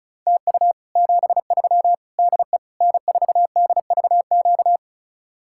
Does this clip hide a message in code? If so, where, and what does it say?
Yes: Morse code from 0.3 to 4.8 s, heard as a keyed tone, about 40 wpm, 700 Hz. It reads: TU 73 DE N4DVQ